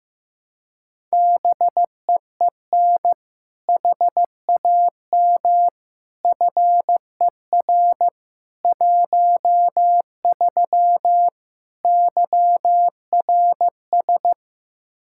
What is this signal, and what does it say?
Morse code, heard as a keyed tone, about 15 wpm, 700 Hz. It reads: BEEN HAM FER 13 YRS
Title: Practice QSO - Exchange 3 - More Info - 15wpm